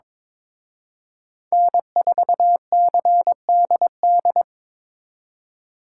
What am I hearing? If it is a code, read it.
Morse code, heard as a keyed tone, about 26 wpm, 700 Hz. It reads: N4CDD
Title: Speed Racing - USA 1x3 Call Signs 22wpm